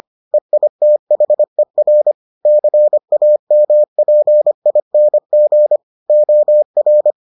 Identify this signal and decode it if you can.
Morse code, heard as a keyed tone, about 25 wpm, 600 Hz. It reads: EITHER CAMPING OR